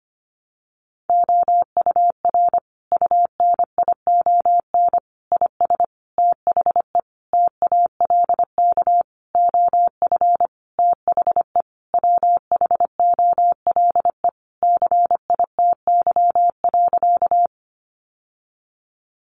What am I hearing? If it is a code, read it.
Morse code, heard as a keyed tone, about 25 wpm, 700 Hz. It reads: OUR UNION IS THE TALK OF THE WHOLE CITY.
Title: Sentences from Top 1400 Words 25wpm